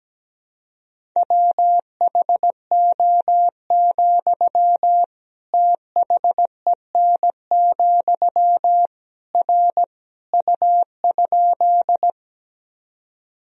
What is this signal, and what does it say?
Morse code, heard as a keyed tone, about 17 wpm, 700 Hz. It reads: WHO, THEN, R U?
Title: Sentences from Top 100 Words and Abbreviations 17wpm